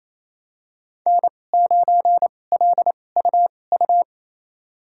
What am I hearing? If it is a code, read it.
Morse code, heard as a keyed tone, about 28 wpm, spaced as for 20 wpm, 700 Hz. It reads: N9LUU
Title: USA Call Signs - 5 Characters - 2x Character Spacing - 28wpm